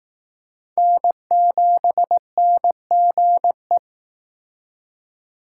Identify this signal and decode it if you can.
Morse code, heard as a keyed tone, about 19 wpm, 700 Hz. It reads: N7NGE